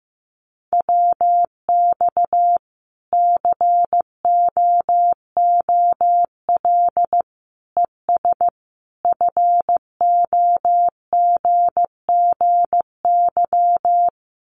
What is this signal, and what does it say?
Morse code, heard as a keyed tone, about 15 wpm, 700 Hz. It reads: WX COOL ES FOGGY
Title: Common QSO Phrases 15wpm